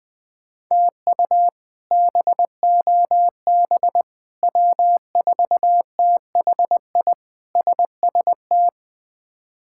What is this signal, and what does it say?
Morse code, heard as a keyed tone, about 19 wpm, 700 Hz. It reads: TU BOB W4THI SST
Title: K1USN SST QSO Practice - 20wpm